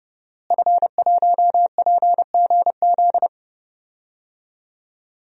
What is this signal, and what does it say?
Morse code, heard as a keyed tone, about 30 wpm, 700 Hz. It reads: F1PGZ
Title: DXCC-100 Call Signs - 5 Characters - Repeated 2x - 30wpm